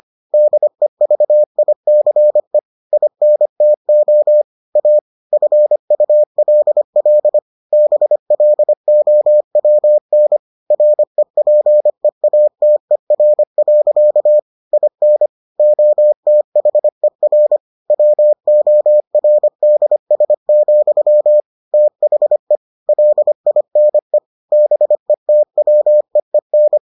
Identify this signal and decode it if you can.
Morse code, heard as a keyed tone, about 25 wpm, 600 Hz. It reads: DEVICE INTO A FULL BLOWN REPEATER. IN OTHER WORDS, THE LINE BETWEEN